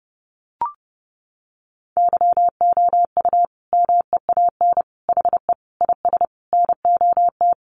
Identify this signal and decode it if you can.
Morse code, heard as a keyed tone, about 30 wpm, 700 Hz. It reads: YOU MEAN HE IS NOT
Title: Sentences from Top 1200 Words 30wpm